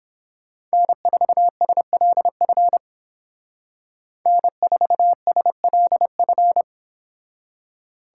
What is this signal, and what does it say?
Morse code, heard as a keyed tone, about 28 wpm, 700 Hz. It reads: N4SLF N4SLF